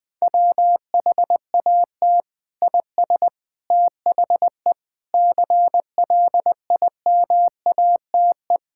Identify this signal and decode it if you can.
Morse code, heard as a keyed tone, about 20 wpm, 700 Hz. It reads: WHAT IS THE CLIMATE
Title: Sentences from Top 1600 Words 20wpm